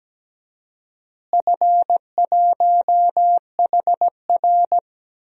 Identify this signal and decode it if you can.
Morse code, heard as a keyed tone, about 17 wpm, 700 Hz. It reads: F1HR
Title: DXCC-100 Call Signs - 4 Characters - Repeated 3x - 17wpm